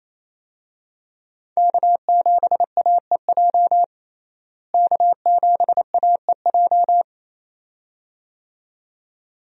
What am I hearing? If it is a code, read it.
Morse code, heard as a keyed tone, about 28 wpm, 700 Hz. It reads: K7AEJ K7AEJ
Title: USA Call Signs - 5 Characters - Repeated 2x - 28wpm